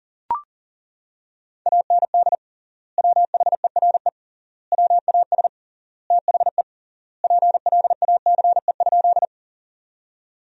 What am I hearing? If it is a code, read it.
Morse code, heard as a keyed tone, about 40 wpm, 700 Hz. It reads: AND WHERE WAS THE PLACE?
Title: 3x Word-Spacing - Sentences from Top 200 Words - 40wpm